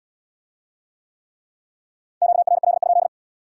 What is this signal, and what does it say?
Morse code, heard as a keyed tone, about 80 wpm, 700 Hz. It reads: QRL?